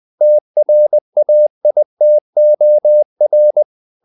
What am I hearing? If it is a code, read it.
Morse code, heard as a keyed tone, about 20 wpm, 600 Hz. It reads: TRAITOR
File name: the old man and the sea part 2 20wpm